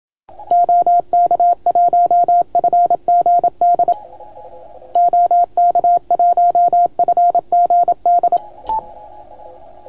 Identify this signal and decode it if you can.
Morse code, heard as a keyed tone, about 25 wpm, 690 Hz. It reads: OK1FGD OK1FGD